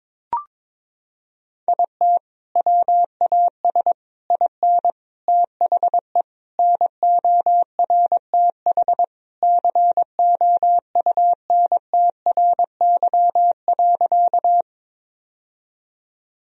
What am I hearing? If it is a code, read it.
Morse code, heard as a keyed tone, about 22 wpm, 700 Hz. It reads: IT WAS IN THE NORTH COUNTRY.
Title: Sentences from Top 1000 Words - Review Part 1 - 22wpm